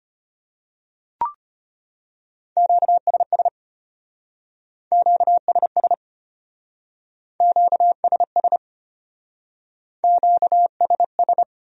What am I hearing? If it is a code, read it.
Morse code, heard as a keyed tone, about 32 wpm, 700 Hz. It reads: QSS QSS QSS QSS